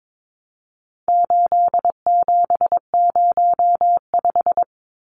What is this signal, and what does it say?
Morse code, heard as a keyed tone, about 22 wpm, 700 Hz. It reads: 8705
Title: Numbers - 4-Digit 22wpm